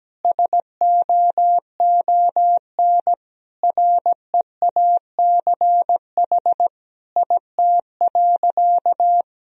Morse code, heard as a keyed tone, about 17 wpm, 700 Hz. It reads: SOON REACH IT.
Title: Sentences from Top 400 Words and Abbreviations 17wpm